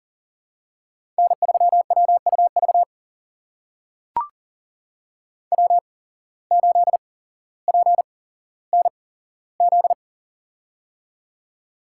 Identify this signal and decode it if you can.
Morse code, heard as a keyed tone, about 40 wpm, spaced as for 17 wpm, 700 Hz. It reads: N3WUV W8PNZ
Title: USA Call Signs - 5 Characters - 8x Character Spacing - 40wpm